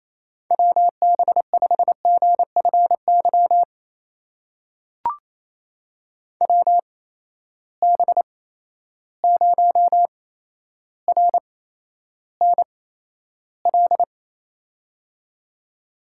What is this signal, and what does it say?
Morse code, heard as a keyed tone, about 28 wpm, spaced as for 12 wpm, 700 Hz. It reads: WB5GFY WB0RNL
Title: USA Call Signs - 6 Characters - 8x Character Spacing - 28wpm